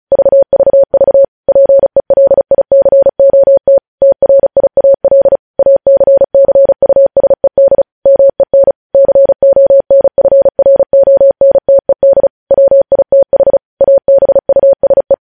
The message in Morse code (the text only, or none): VVV PELICOT TRIAL ACCUSED MEN CONFRONTED WITH ABUSE